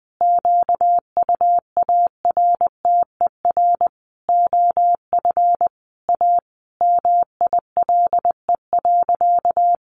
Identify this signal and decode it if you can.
Morse code, heard as a keyed tone, about 20 wpm, 700 Hz. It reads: QUARTER OF A MILE.
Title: Sentences from Top 1300 Words 20wpm